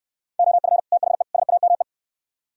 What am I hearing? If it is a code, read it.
Morse code, heard as a keyed tone, about 67 wpm, 700 Hz. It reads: OF THE SAME